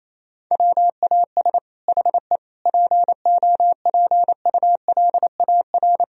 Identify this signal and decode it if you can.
Morse code, heard as a keyed tone, about 28 wpm, 700 Hz. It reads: WAS HE POPULAR
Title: Sentences from Top 1000 Words - Review Part 3 - 28wpm